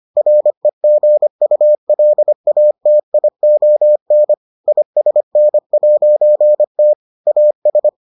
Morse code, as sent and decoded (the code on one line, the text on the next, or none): .-. --. ..- .-.. .- - .. --- -. / .. ... -. .----. - / .- ...
REGULATION ISN'T AS